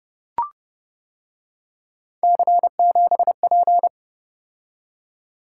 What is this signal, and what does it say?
Morse code, heard as a keyed tone, about 30 wpm, 700 Hz. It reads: C7P